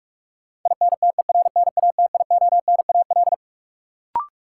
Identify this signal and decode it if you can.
Morse code, heard as a keyed tone, about 46 wpm, 700 Hz. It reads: INTERNATIONAL
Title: Sentences from Top 1000 Words - Review Part 4 - New Word Drill - 45wpm